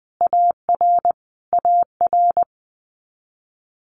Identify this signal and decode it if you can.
Morse code, heard as a keyed tone, about 20 wpm, 700 Hz. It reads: AR AR
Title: US States Repeated 3x 20wpm